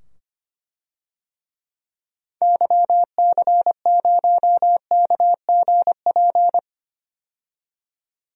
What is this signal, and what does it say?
Morse code, heard as a keyed tone, about 25 wpm, 700 Hz. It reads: YC0KGP
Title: DXCC-100 Call Signs - 6 Characters - 25wpm